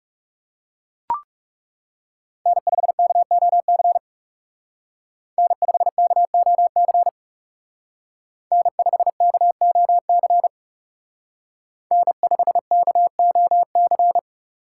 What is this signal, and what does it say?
Morse code, heard as a keyed tone, about 35 wpm, 700 Hz. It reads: N5KOC N5KOC N5KOC N5KOC
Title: Speed Racing - USA 1x3 Call Signs 30wpm